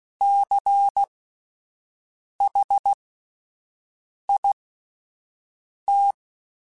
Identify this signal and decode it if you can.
Morse code, heard as a keyed tone, about 16 wpm, spaced as for 5 wpm, 800 Hz. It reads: CHIT
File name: Spaced Alphas / Spaced Alphas mp3s / Cs sp @10